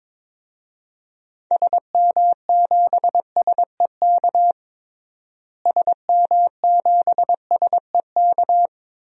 Morse code, heard as a keyed tone, about 22 wpm, 700 Hz. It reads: SM7SEK SM7SEK
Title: DXCC-100 Call Signs - 6 Characters - Repeated 3x - 22wpm